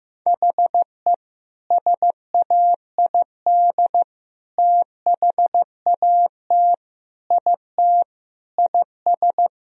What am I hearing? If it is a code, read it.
Morse code, heard as a keyed tone, about 16 wpm, 700 Hz. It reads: HE SAID THAT IT IS
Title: Sentences from Top 1000 Words - Review Part 2 - 15wpm